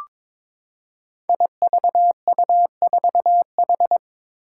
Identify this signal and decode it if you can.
Morse code, heard as a keyed tone, about 22 wpm, 700 Hz. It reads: IVU4H
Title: Five Challenging Letter-Number Combinations 22wpm